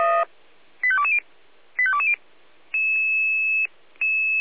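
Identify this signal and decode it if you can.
Five-tone selective call (ZVEI), heard as quick runs of five tones, tones of about 70 ms. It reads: 85220 85220
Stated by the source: ZVEI five-tone selective call received on an SDR.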